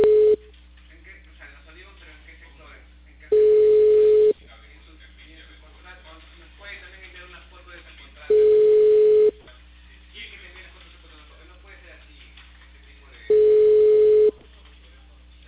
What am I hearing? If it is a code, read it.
Ringback tone (Europe, 425 Hz), with speech.